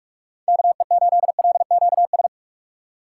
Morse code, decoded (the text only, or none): KE8LQS